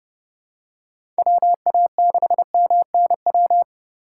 Morse code, heard as a keyed tone, about 30 wpm, 700 Hz. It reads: WA6MNW
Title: USA Call Signs - 6 Characters - 2x Character Spacing - 30wpm